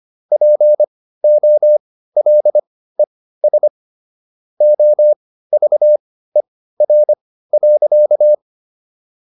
Morse code, heard as a keyed tone, about 25 wpm, spaced as for 15 wpm, 600 Hz. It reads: POLES OVER.